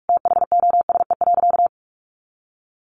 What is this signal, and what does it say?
Morse code, heard as a keyed tone, about 44 wpm, 700 Hz. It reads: THOSE.
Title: A Sound Of Thunder 45wpm